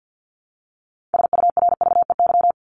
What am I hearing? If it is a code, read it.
Morse code, heard as a keyed tone, about 48 wpm, 700 Hz. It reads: SURVEY
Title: Top 1501-1750 Words 50wpm